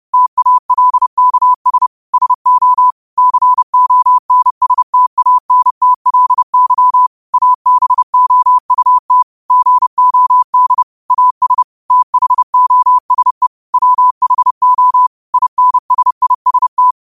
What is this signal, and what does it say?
Morse code, heard as a keyed tone, about 30 wpm, 1000 Hz. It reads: TALKS SO CONSTANTLY ABOUT GOD AS THOSE WHO INSIST